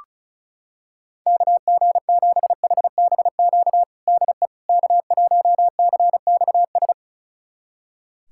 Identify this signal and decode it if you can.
Morse code, heard as a keyed tone, about 36 wpm, 700 Hz. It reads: KG7HBQ DE K1CXS
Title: USA Callsign DE Callsign 35wpm